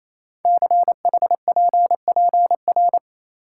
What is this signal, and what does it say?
Morse code, heard as a keyed tone, about 29 wpm, 700 Hz. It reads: CHPPR
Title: Code Groups - 5 Letters - 28wpm